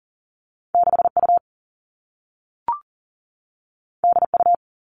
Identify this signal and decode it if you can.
Morse code, heard as a keyed tone, about 40 wpm, 700 Hz. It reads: 6U DU
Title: All Cut Numbers - 2-Digit 40wpm